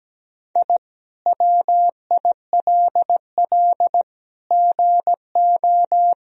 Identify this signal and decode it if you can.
Morse code, heard as a keyed tone, about 17 wpm, 700 Hz. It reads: I WILL GO
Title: Sentences from Top 1000 Words - Review Part 4 - 17wpm